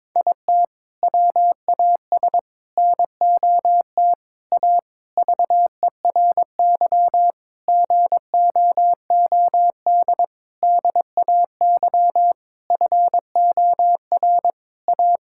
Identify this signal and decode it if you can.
Morse code, heard as a keyed tone, about 22 wpm, 700 Hz. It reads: IT WAS NOT A VERY GOOD DAY FOR A
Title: Sentences from Top 1000 Words - Review Part 6 - 22wpm